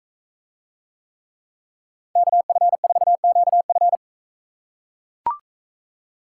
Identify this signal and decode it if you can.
Morse code, heard as a keyed tone, about 44 wpm, 700 Hz. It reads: KF4QF